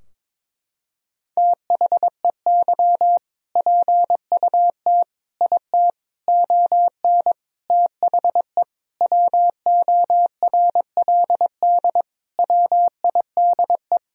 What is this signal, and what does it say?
Morse code, heard as a keyed tone, about 22 wpm, 700 Hz. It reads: THEY PUT IT ON THE WORLD WIDE
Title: Sentences from Top 1000 Words - Review Part 4 - 22wpm